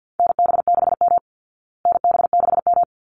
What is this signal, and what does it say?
Morse code, heard as a keyed tone, about 50 wpm, 700 Hz. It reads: NB6M NB6M
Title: Call Signs Repeated 3x - 50wpm